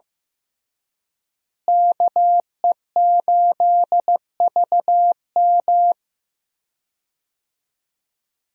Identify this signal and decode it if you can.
Morse code, heard as a keyed tone, about 15 wpm, 700 Hz. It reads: KE8VM
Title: USA Call Signs - 5 Characters - Repeated 3x - 15wpm